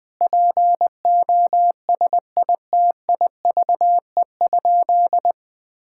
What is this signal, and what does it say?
Morse code, heard as a keyed tone, about 20 wpm, 700 Hz. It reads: POSITIVE?
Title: Sentences from Top 1000 Words - Review Part 5 - 20wpm